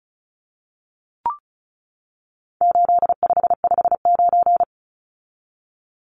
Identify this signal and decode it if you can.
Morse code, heard as a keyed tone, about 34 wpm, 700 Hz. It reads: 8559
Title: Numbers - 4-Digit 35wpm